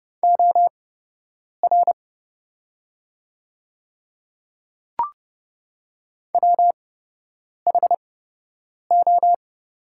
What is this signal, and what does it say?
Morse code, heard as a keyed tone, about 30 wpm, spaced as for 8 wpm, 700 Hz. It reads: OR WHO
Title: 8x Character Spacing - Top 100 Words - 30wpm